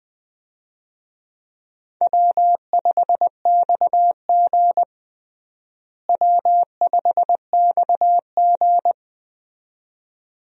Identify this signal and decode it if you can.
Morse code, heard as a keyed tone, about 20 wpm, 700 Hz. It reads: W5XG W5XG
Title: USA Call Signs - 4 Characters - Repeated 2x - 20wpm